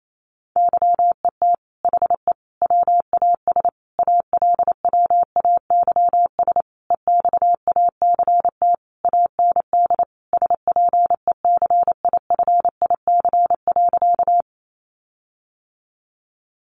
Morse code, heard as a keyed tone, about 28 wpm, 700 Hz. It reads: YET HE WAS ALWAYS EXACT AND SPECIFIC.
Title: Sentences from Top 1300 Words 28wpm